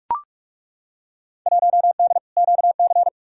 Morse code, heard as a keyed tone, about 43 wpm, 700 Hz. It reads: 1D QC
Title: Field Day Exchanges Encore - 45wpm